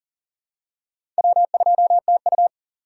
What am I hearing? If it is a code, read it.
Morse code, heard as a keyed tone, about 40 wpm, 700 Hz. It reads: W2TU
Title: USA Call Signs - 4 Characters - 6x Character Spacing - 40wpm